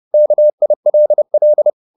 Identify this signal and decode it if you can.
Morse code, heard as a keyed tone, about 30 wpm, 600 Hz. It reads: KILL